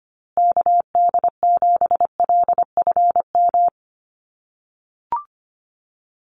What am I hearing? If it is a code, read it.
Morse code, heard as a keyed tone, about 25 wpm, 700 Hz. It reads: KD7LFM
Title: USA Call Signs Repeated 2x 25wpm